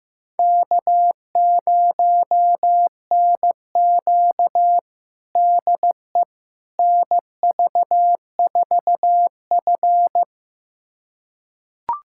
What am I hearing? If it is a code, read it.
Morse code, heard as a keyed tone, about 15 wpm, 700 Hz. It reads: K0NQ DE NV4F
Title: USA Callsign DE Callsign - 4 Characters - 15wpm